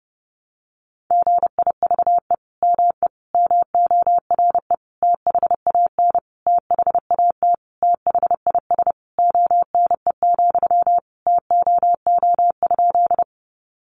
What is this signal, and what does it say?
Morse code, heard as a keyed tone, about 30 wpm, 700 Hz. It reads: GIVE ME MORE THAN THAT THIS ONE, TOO?
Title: Sentences from Top 200 Words 30wpm